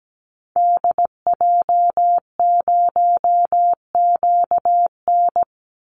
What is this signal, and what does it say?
Morse code, heard as a keyed tone, about 17 wpm, 700 Hz. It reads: DJ0QN
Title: Call Signs Repeated 3x - 17wpm